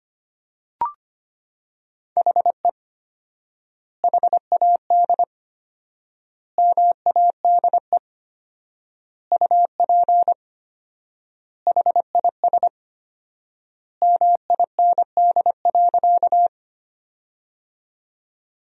Morse code, heard as a keyed tone, about 25 wpm, 700 Hz. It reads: HE HAD MADE UP HIS MIND.
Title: 4x Word-Spacing - Sentences from Top 500 Words - 25wpm